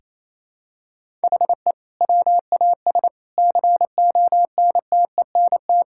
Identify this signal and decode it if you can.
Morse code, heard as a keyed tone, about 28 wpm, 700 Hz. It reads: HE WAS CONTENT
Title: Sentences from Top 1000 Words - Review Part 6 - 28wpm